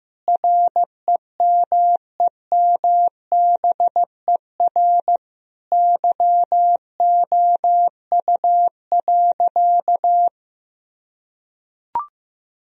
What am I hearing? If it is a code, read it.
Morse code, heard as a keyed tone, about 15 wpm, 700 Hz. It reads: REMEMBER YOU.
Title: Sentences from Top 1000 Words - Review Part 2 - 15wpm